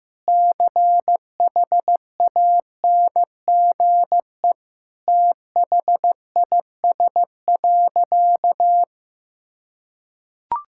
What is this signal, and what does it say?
Morse code, heard as a keyed tone, about 15 wpm, 700 Hz. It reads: CHANGE THIS.
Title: Sentences from Top 1000 Words - Review Part 6 - 15wpm